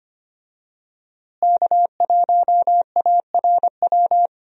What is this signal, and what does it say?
Morse code, heard as a keyed tone, about 25 wpm, 700 Hz. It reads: K1ARW
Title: USA Call Signs - 5 Characters - Repeated 3x - 25wpm